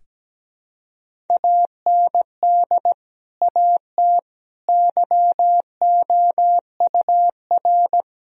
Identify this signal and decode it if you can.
Morse code, heard as a keyed tone, about 17 wpm, 700 Hz. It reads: AND AT YOUR
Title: Sentences from Top 1000 Words - Review Part 1 - 17wpm